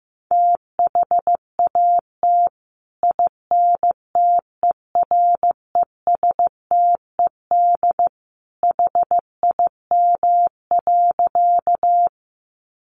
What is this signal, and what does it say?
Morse code, heard as a keyed tone, about 15 wpm, 700 Hz. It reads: THAT INTERESTED HIM.
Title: Sentences from Top 900 Words 15wpm